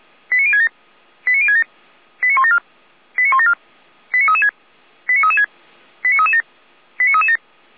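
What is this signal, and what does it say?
Five-tone selective call (ZVEI), heard as quick runs of five tones, tones of about 70 ms. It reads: 89067 89067 89174 89174 89336 89336 89337 89337
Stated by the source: ZVEI five-tone call 89246 received by SDR.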